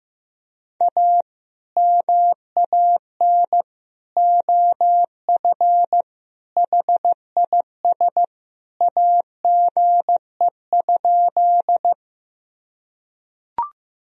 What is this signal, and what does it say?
Morse code, heard as a keyed tone, about 15 wpm, 700 Hz. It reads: A MAN OF HIS AGE?
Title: Sentences from Top 1000 Words - Review Part 3 - 15wpm